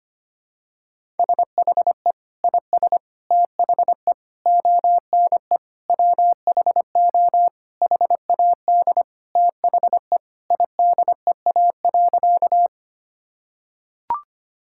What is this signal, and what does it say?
Morse code, heard as a keyed tone, about 25 wpm, 700 Hz. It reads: SHE IS THE ONE WHO HAD THE IDEA.
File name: practice / 25wpm / Sentences with 2-Word Phrases - 25wpm